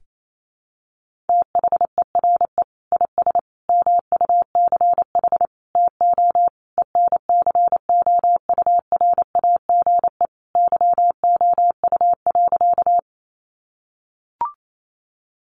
Morse code, heard as a keyed tone, about 28 wpm, 700 Hz. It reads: THERE IS MUCH TO ENCOURAGE YOU.
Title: Sentences from Top 1400 Words 28wpm